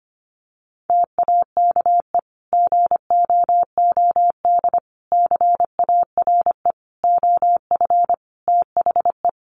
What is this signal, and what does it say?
Morse code, heard as a keyed tone, about 25 wpm, 700 Hz. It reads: TAKE GOOD CARE OF THE
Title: Sentences from Top 400 Words 25wpm